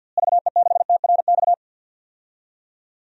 Morse code, heard as a keyed tone, about 52 wpm, 700 Hz. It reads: VE6TRX
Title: DXCC-100 Call Signs - 6 Characters - Repeated 2x - 50wpm